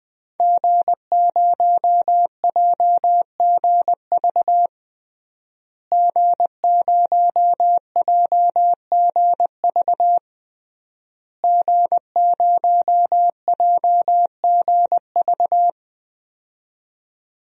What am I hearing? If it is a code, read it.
Morse code, heard as a keyed tone, about 21 wpm, 700 Hz. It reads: G0JGV G0JGV G0JGV
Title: DXCC-100 Call Signs - 5 Characters - Repeated 3x - 20wpm